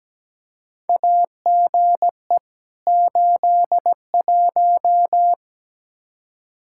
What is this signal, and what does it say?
Morse code, heard as a keyed tone, about 17 wpm, 700 Hz. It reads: AGE 81
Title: Practice QSO - Exchange 3 - More Info - 17wpm